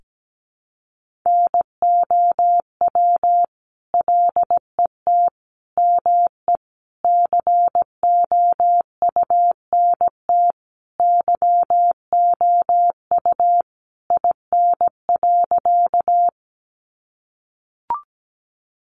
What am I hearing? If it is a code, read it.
Morse code, heard as a keyed tone, about 17 wpm, 700 Hz. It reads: NOW LET ME COUNT YOU IN.
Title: Sentences from Top 1500 Words 17wpm